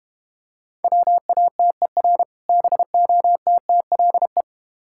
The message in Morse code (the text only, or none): WATER BOTTLE